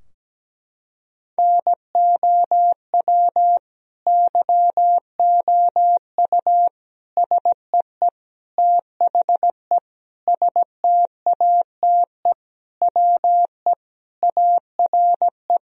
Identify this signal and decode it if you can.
Morse code, heard as a keyed tone, about 17 wpm, 700 Hz. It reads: NOW YOU SEE THE STATE WE ARE
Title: Sentences from Top 1000 Words - Review Part 2 - 17wpm